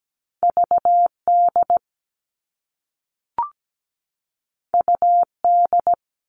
Morse code, heard as a keyed tone, about 17 wpm, 700 Hz. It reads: VD UD
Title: All Cut Numbers - 2-Digit 17wpm